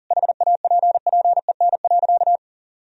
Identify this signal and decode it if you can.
Morse code, heard as a keyed tone, about 40 wpm, 700 Hz. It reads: HAPPEN.
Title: Sentences from Top 1000 Words - Review Part 4 - 40wpm